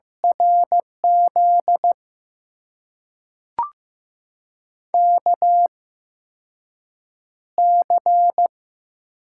Morse code, heard as a keyed tone, about 15 wpm, spaced as for 6 wpm, 700 Hz. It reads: RZ KC
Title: USA Call Signs - 6 Characters - 8x Character Spacing - 15wpm